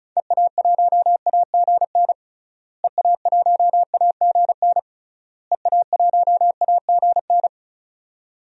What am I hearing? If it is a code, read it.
Morse code, heard as a keyed tone, about 36 wpm, 700 Hz. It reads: EA1AGN EA1AGN EA1AGN
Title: DXCC-100 Call Signs - 6 Characters - Repeated 3x - 35wpm